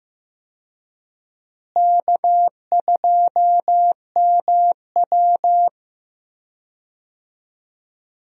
Morse code, heard as a keyed tone, about 15 wpm, 700 Hz. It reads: K2MW
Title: USA Call Signs - 4 Characters - Repeated 2x - 15wpm